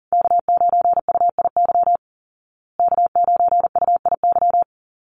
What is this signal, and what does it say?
Morse code, heard as a keyed tone, about 40 wpm, 700 Hz. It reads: K9UIY K9UIY
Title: Call Signs Repeated 3x - 40wpm